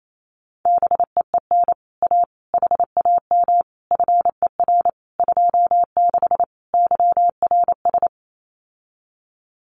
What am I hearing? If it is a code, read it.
Morse code, heard as a keyed tone, about 28 wpm, 700 Hz. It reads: BEEN A HAM FER 26 YRS
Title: Common QSO Phrases Encore 28wpm